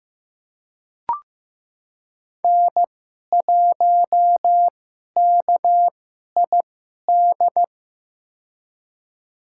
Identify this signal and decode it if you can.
Morse code, heard as a keyed tone, about 15 wpm, spaced as for 11 wpm, 700 Hz. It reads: N1KID
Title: USA Call Signs - 5 Characters - 2x Character Spacing - 15wpm